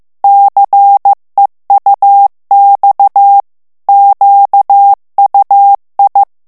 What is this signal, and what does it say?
Morse code, heard as a keyed tone, about 15 wpm, 800 Hz. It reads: CEUX QUI
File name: FRANCAIS TEXTE CLAIR 2 VITESSE 14 MOTS MINUTES